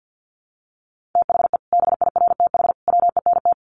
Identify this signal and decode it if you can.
Morse code, heard as a keyed tone, about 50 wpm, 700 Hz. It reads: THE BIRTH WENT
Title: Sentences from Top 800 Words 50wpm